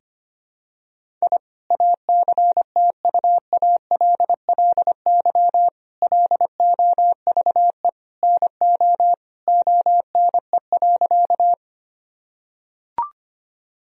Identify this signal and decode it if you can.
Morse code, heard as a keyed tone, about 25 wpm, 700 Hz. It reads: I ACTUALLY LOVE NO ONE.
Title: Sentences from Top 1000 Words - Review Part 4 - 25wpm